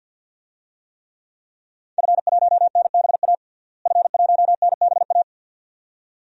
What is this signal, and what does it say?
Morse code, heard as a keyed tone, about 52 wpm, 700 Hz. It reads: F1NBA F1NBA
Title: DXCC-100 Call Signs - 5 Characters - Repeated 2x - 50wpm